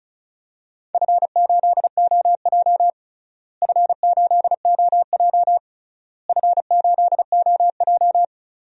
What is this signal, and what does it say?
Morse code, heard as a keyed tone, about 36 wpm, 700 Hz. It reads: F8OJ F8OJ F8OJ
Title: DXCC-100 Call Signs - 4 Characters - Repeated 3x - 35wpm